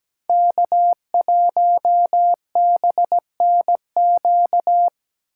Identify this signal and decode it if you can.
Morse code, heard as a keyed tone, about 17 wpm, 700 Hz. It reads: K1BNQ
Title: USA Call Signs - 5 Characters - 4x Character Spacing - 17wpm